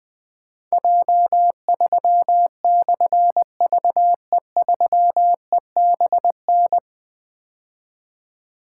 Morse code, heard as a keyed tone, about 20 wpm, 700 Hz. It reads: J3/VE3EBN
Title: Difficult Call Signs 20wpm